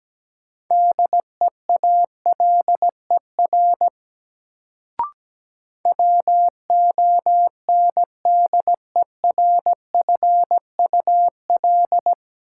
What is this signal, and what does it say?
Morse code, heard as a keyed tone, about 17 wpm, 700 Hz. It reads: DEALER WONDERFUL
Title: Sentences from Top 1100 Words - New Word Drill - 17wpm